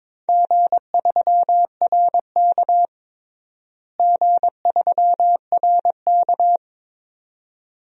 Morse code, heard as a keyed tone, about 21 wpm, 700 Hz. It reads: G3RK G3RK
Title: DXCC-100 Call Signs - 4 Characters - Repeated 2x - 22wpm